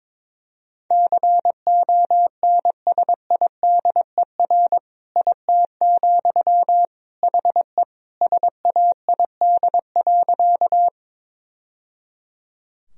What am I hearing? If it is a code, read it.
Morse code, heard as a keyed tone, about 21 wpm, 700 Hz. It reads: CONSIDER IT, HE SAID.
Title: Sentences from Top 1000 Words - Review Part 4 - 22wpm